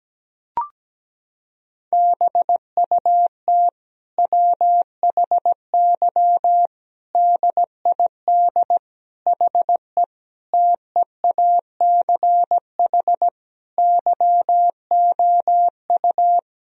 Morse code, heard as a keyed tone, about 17 wpm, 700 Hz. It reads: BUT WHY DID HE TEACH YOU
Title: Sentences from Top 1000 Words - Review Part 2 - 17wpm